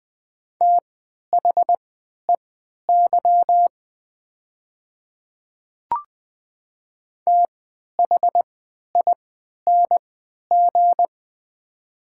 Morse code, heard as a keyed tone, about 20 wpm, spaced as for 11 wpm, 700 Hz. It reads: THEY THING
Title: 3x Character Spacing - Top 100 Words - 20wpm